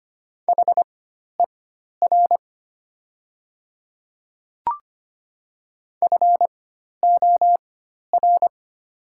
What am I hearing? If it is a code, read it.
Morse code, heard as a keyed tone, about 25 wpm, spaced as for 12 wpm, 700 Hz. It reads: HER FOR
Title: 4x Character Spacing - Top 100 Words - 25wpm